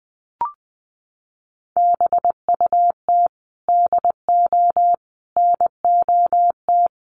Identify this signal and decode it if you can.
Morse code, heard as a keyed tone, about 20 wpm, 700 Hz. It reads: BUT DO NOT
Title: Sentences from Top 1500 Words 20wpm